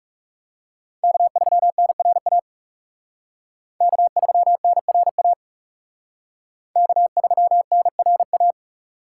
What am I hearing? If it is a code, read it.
Morse code, heard as a keyed tone, about 37 wpm, 700 Hz. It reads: K3NRA K3NRA K3NRA
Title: Speed Racing - USA 1x3 Call Signs 30wpm